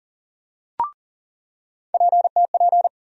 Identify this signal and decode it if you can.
Morse code, heard as a keyed tone, about 40 wpm, 700 Hz. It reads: PTP
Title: USA Call Sign Postfixes - 3 Letters 40wpm